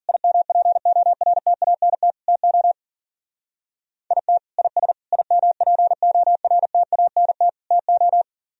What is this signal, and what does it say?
Morse code, heard as a keyed tone, about 50 wpm, 700 Hz. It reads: IMPORTANT TO IT IS IMPORTANT TO
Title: Speed-Racing - Sets of 4 Words 40wpm